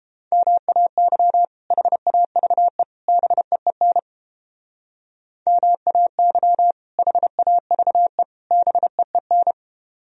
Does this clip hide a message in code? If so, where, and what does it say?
Yes: Morse code from 0.3 to 9.5 s, heard as a keyed tone, about 29 wpm, 700 Hz. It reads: MAY HAVE BEEN MAY HAVE BEEN